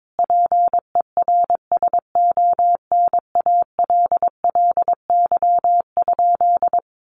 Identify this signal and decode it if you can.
Morse code, heard as a keyed tone, about 22 wpm, 700 Hz. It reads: PERSONALLY?
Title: Sentences from Top 1400 Words 22wpm